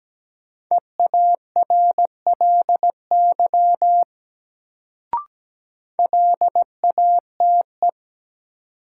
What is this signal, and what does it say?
Morse code, heard as a keyed tone, about 17 wpm, 700 Hz. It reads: EARLY LATE
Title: Sets of 3 Words - New Word Drill - 17wpm